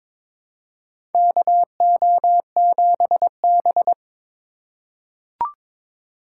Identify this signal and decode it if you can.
Morse code, heard as a keyed tone, about 22 wpm, 700 Hz. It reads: KO7B
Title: USA Call Signs - 2x Character Spacing - 22wpm